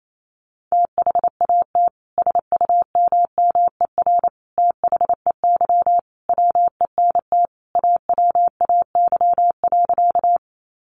Morse code, heard as a keyed tone, about 28 wpm, 700 Hz. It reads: THAT SUMMER THEY WENT AWAY.
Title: Sentences from Top 1100 Words 28wpm